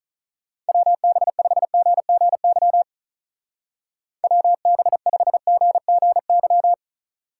Morse code, heard as a keyed tone, about 44 wpm, 700 Hz. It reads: WB5GGY WB5GGY
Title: Speed Racing - USA 2x3 Call Signs 35wpm